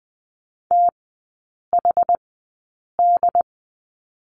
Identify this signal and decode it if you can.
Morse code, heard as a keyed tone, about 20 wpm, spaced as for 8 wpm, 700 Hz. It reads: THD
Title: 067 Lesson 12e - Letter d - Three Characters 20wpm